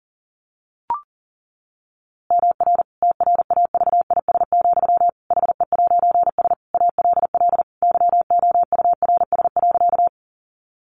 Morse code, heard as a keyed tone, about 40 wpm, 700 Hz. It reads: MR TRAVIS, HE'S ALL YOURS.
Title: A Sound Of Thunder 40wpm